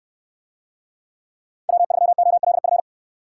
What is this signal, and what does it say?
Morse code, heard as a keyed tone, about 64 wpm, 700 Hz. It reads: W3OLF